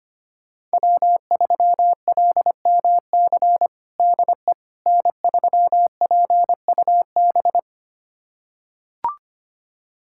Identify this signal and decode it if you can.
Morse code, heard as a keyed tone, about 25 wpm, 700 Hz. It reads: W3LMC DE N3PUB
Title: USA Callsign DE Callsign - 5 Characters - 25wpm